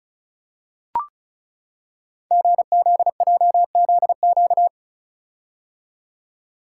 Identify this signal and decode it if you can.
Morse code, heard as a keyed tone, about 35 wpm, 700 Hz. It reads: GZJZQ
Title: Code Groups - 5 Letter:Numbers - 35wpm